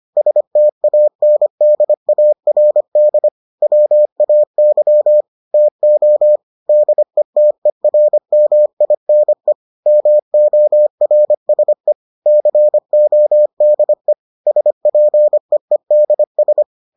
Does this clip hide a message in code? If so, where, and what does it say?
Yes: Morse code from 0.2 to 16.6 s, heard as a keyed tone, about 25 wpm, 600 Hz. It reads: STANDARD WAY TO DETERMINE MORSE CODE SPEEDS